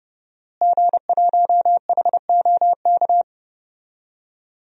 Morse code, heard as a keyed tone, about 30 wpm, 700 Hz. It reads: G1HOK
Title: DXCC-100 Call Signs - 5 Characters - Repeated 3x - 30wpm